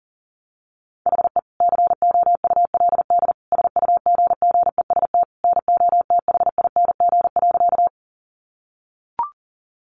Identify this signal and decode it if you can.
Morse code, heard as a keyed tone, about 37 wpm, 700 Hz. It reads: HE COULD SUGGEST NOTHING.
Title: Sentences from Top 1300 Words 40wpm